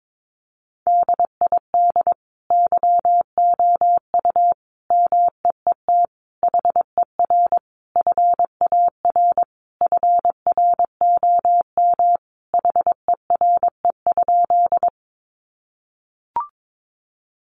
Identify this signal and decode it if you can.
Morse code, heard as a keyed tone, about 22 wpm, 700 Hz. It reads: DID YOU MEET HER FAR FROM HERE?
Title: Sentences from Top 1100 Words 22wpm